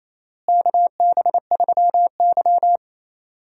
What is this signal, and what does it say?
Morse code, heard as a keyed tone, about 28 wpm, 700 Hz. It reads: KB3Y
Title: USA Call Signs - 4 Characters - Repeated 3x - 28wpm